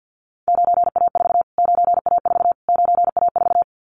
Morse code, heard as a keyed tone, about 48 wpm, 700 Hz. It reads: 9A4 9A4 9A4
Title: Callsign Prefixes for Top 25 DXCC - Repeated 3x - 50wpm